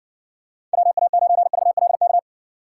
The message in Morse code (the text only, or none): WA9VLK